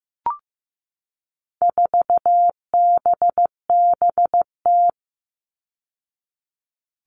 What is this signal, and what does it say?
Morse code, heard as a keyed tone, about 15 wpm, 700 Hz. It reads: 4BBT
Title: All Cut Numbers - 4-Digit 15wpm